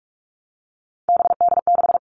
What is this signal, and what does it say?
Morse code, heard as a keyed tone, about 44 wpm, 700 Hz. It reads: BD6
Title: All Cut Numbers - 3-Digit 45wpm